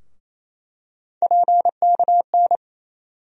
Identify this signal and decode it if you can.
Morse code, heard as a keyed tone, about 28 wpm, 700 Hz. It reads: PKN
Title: USA Call Sign Postfixes - 3 Letters 28wpm